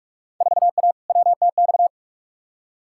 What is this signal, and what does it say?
Morse code, heard as a keyed tone, about 46 wpm, 700 Hz. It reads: 4A WTX
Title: Field Day Exchanges - 45wpm